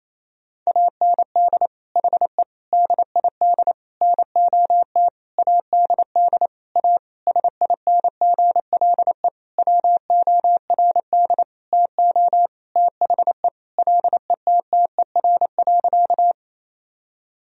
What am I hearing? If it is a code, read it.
Morse code, heard as a keyed tone, about 28 wpm, 700 Hz. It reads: AND HE DID NOT ADD A SINGLE WORD TO THE LETTER.
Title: Sentences from Top 1000 Words - Review Part 1 - 28wpm